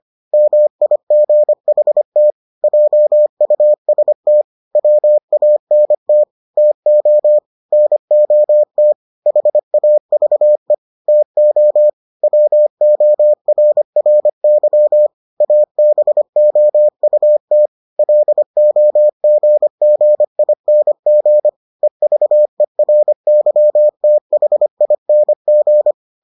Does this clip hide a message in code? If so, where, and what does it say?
Yes: Morse code from 0.3 to 25.9 s, heard as a keyed tone, about 25 wpm, 600 Hz. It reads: MIGHT JUST WANT TO NOT HAVE TO WORRY ABOUT LOGGING EVERYTHING